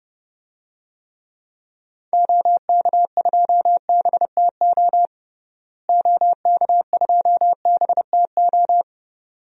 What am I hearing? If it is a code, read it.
Morse code, heard as a keyed tone, about 30 wpm, 700 Hz. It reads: OK2BTO OK2BTO
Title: DXCC-100 Call Signs - 6 Characters - Repeated 2x - 30wpm